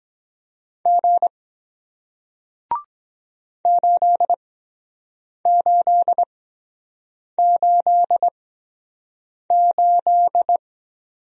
Morse code, heard as a keyed tone, about 23 wpm, spaced as for 7 wpm, 700 Hz. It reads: G 8888